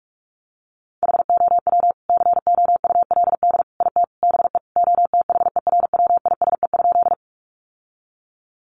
Morse code, heard as a keyed tone, about 44 wpm, 700 Hz. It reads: HOW COULD IT BE OTHERWISE?
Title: Sentences from Top 1200 Words 45wpm